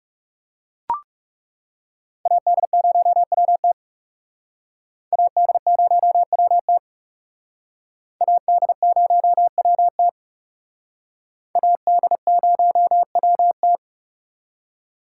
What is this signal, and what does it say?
Morse code, heard as a keyed tone, about 37 wpm, 700 Hz. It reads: AD0WT AD0WT AD0WT AD0WT
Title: Speed Racing - USA 2x2 Call Signs 30wpm